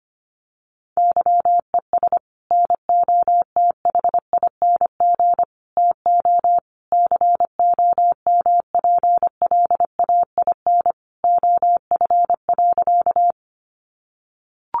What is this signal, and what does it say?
Morse code, heard as a keyed tone, about 25 wpm, 700 Hz. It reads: YES NOTHING TO COMPLAIN OF.
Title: Sentences from Top 1300 Words 25wpm